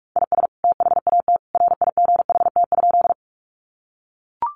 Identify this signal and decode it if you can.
Morse code, heard as a keyed tone, about 44 wpm, 700 Hz. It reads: IS THAT RIGHT?
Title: Sentences from Top 200 Words 45wpm